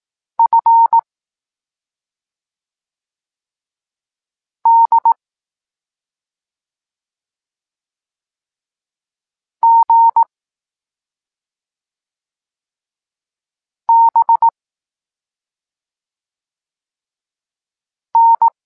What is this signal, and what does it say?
Morse code, heard as a keyed tone, about 18 wpm, 920 Hz. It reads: F D G B N
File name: Tekst-26 3